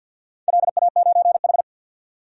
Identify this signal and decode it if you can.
Morse code, heard as a keyed tone, about 51 wpm, 700 Hz. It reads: LA9H
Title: DXCC-100 Call Signs - 4 Characters - 50wpm